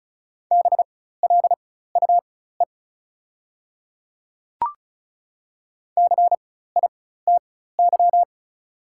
Morse code, heard as a keyed tone, about 34 wpm, spaced as for 16 wpm, 690 Hz. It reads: BLUE CITY
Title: 4x Character Spacing - Top 500 Words - 35wpm